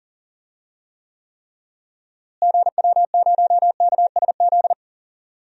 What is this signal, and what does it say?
Morse code, heard as a keyed tone, about 40 wpm, 700 Hz. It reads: GW0KSZ
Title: DXCC-100 Call Signs - 6 Characters - Repeated 3x - 40wpm